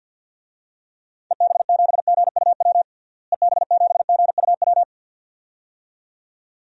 Morse code, heard as a keyed tone, about 47 wpm, 700 Hz. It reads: EB7GUW EB7GUW
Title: DXCC-100 Call Signs - 6 Characters - Repeated 2x - 50wpm